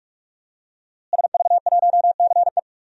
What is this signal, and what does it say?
Morse code, heard as a keyed tone, about 46 wpm, 700 Hz. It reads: SV1CE
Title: DXCC-100 Call Signs - 5 Characters - Repeated 2x - 45wpm